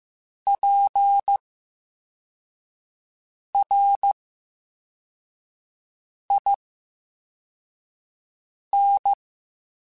Morse code, heard as a keyed tone, about 15 wpm, spaced as for 4 wpm, 800 Hz. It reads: PRIN
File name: Special INGs-mp3s / Special INGs-mp3s / VerySlowToFaster Ing @13